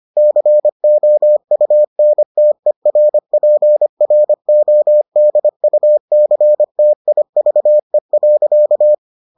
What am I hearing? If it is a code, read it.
Morse code, heard as a keyed tone, about 25 wpm, 600 Hz. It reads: COUNTERPRODUCTIVE.